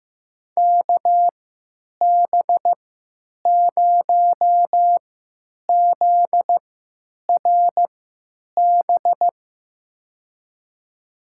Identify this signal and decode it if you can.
Morse code, heard as a keyed tone, about 15 wpm, spaced as for 9 wpm, 700 Hz. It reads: KB0ZRB
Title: USA Call Signs - 6 Characters - 3x Character Spacing - 15wpm